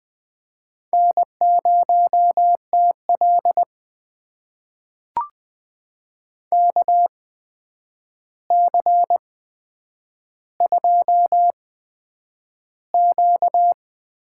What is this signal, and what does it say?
Morse code, heard as a keyed tone, about 20 wpm, spaced as for 9 wpm, 700 Hz. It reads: N0TL KC2Q
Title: USA Call Signs - 8x Character Spacing - 20wpm